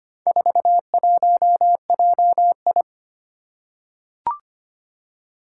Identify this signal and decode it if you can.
Morse code, heard as a keyed tone, about 25 wpm, 700 Hz. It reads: 41JI
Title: Four Challenging Letter-Number Combinations 25wpm